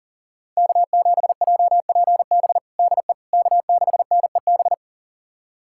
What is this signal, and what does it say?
Morse code, heard as a keyed tone, about 40 wpm, 700 Hz. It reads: K7JPB DE K6NEB
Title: USA Callsign DE Callsign 40wpm